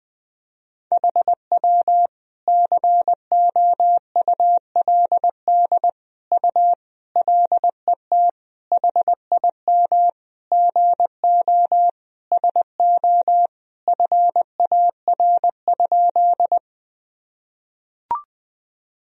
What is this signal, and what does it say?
Morse code, heard as a keyed tone, about 20 wpm, 700 Hz. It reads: HW COULD U LET HIM GO SO FAR?
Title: Sentences from Top 300 Words and Abbreviations 20wpm